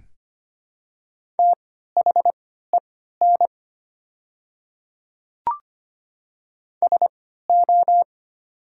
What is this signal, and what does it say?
Morse code, heard as a keyed tone, about 25 wpm, spaced as for 14 wpm, 700 Hz. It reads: THEN SO
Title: 3x Character Spacing - Top 100 Words - 25wpm